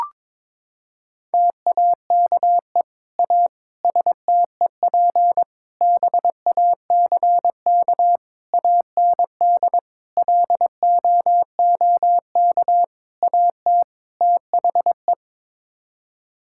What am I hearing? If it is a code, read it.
Morse code, heard as a keyed tone, about 22 wpm, 700 Hz. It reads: TAKE A STEP BACK AND LOOK AT THE
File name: practice / 22wpm / Sets of 8 Words - 22wpm